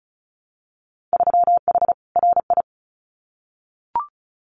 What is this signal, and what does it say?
Morse code, heard as a keyed tone, about 34 wpm, 700 Hz. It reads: 3H RI
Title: Winter Field Day Exchanges Encore 35wpm